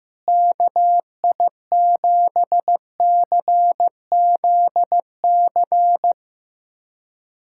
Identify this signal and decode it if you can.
Morse code, heard as a keyed tone, about 15 wpm, 700 Hz. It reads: KI7CZC
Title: USA Callsign DE Callsign - 6 Characters - 15wpm